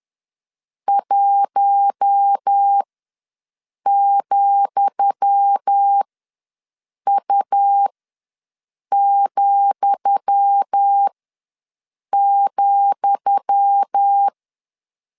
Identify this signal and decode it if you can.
Morse code, heard as a keyed tone, about 10 wpm, spaced as for 6 wpm, 800 Hz. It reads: 1,U,,
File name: LEZIONI / 10 SIGNS / 10D COMA ws 10wpm 4314